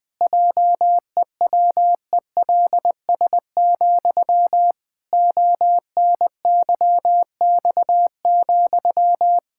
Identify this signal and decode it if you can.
Morse code, heard as a keyed tone, about 20 wpm, 700 Hz. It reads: JEWELS, ONYX,
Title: Pangram Challenge v2 - 20wpm